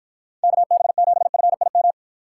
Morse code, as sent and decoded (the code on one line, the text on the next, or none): -.- -... --... ..-. .. --
KB7FIM